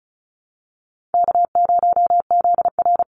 Morse code, heard as a keyed tone, about 34 wpm, 700 Hz. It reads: K0ZR
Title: Call Signs Repeated 3x - 35wpm